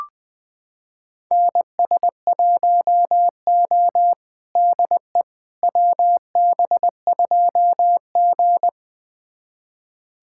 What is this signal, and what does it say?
Morse code, heard as a keyed tone, about 21 wpm, 700 Hz. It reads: NS1O DE WB2G
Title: USA Callsign DE Callsign - 4 Characters - 20wpm